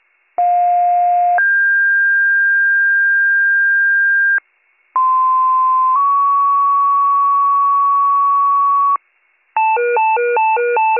Two-tone page, heard as a one-second tone, then a three-second tone, about 710 then 1600 Hz, about 1040 then 1120 Hz, with noise. It is heard from 0.4 to 9.0 s.